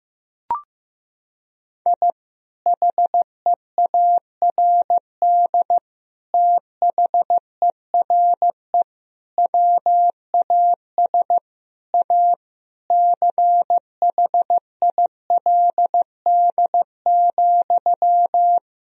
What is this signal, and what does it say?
Morse code, heard as a keyed tone, about 15 wpm, 700 Hz. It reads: I HEARD THERE WAS A CHILD,
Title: Sentences from Top 1000 Words - Review Part 2 - 15wpm